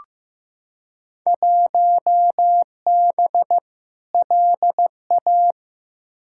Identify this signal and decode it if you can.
Morse code, heard as a keyed tone, about 15 wpm, 700 Hz. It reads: 1B LA
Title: Field Day Exchanges - 15wpm